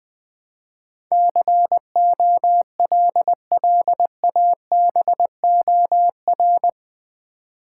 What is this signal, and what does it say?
Morse code, heard as a keyed tone, about 20 wpm, 700 Hz. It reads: COLLABOR
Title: Build A Word - 4 Letters At A Time - 11-Letter Words - 20wpm